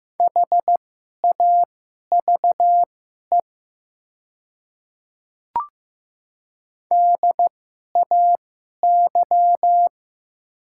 Morse code, heard as a keyed tone, about 15 wpm, spaced as for 11 wpm, 700 Hz. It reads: HAVE DAY
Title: 2x Character Spacing - Top 100 Words - 15wpm